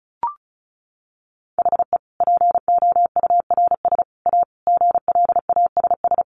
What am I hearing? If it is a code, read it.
Morse code, heard as a keyed tone, about 34 wpm, 700 Hz. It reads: HE POURS A GLASS
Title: Additional Easy Phrases 35wpm